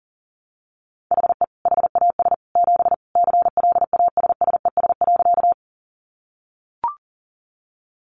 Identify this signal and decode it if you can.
Morse code, heard as a keyed tone, about 40 wpm, 700 Hz. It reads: HE HAS 7 CLASSES.
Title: Additional Easy Phrases 40wpm